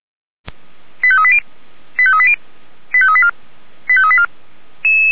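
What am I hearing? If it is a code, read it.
Five-tone selective call (ZVEI), heard as quick runs of five tones, tones of about 70 ms. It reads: 85380 85380 85384 85384
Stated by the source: ZVEI five-tone selective call 89155 received by SDR, logged on the Feuerwehr Orlamünde fire brigade server.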